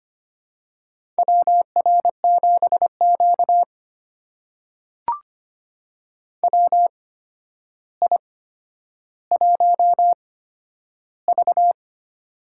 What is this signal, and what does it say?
Morse code, heard as a keyed tone, about 25 wpm, spaced as for 11 wpm, 700 Hz. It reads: WR7Q WI1V
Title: USA Call Signs - 4 Characters - 8x Character Spacing - 25wpm